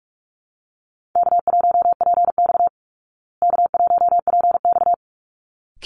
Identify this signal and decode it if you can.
Morse code, heard as a keyed tone, about 44 wpm, 700 Hz. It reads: K1PX K1PX
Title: Easy Call Signs Repeated 2x - 45wpm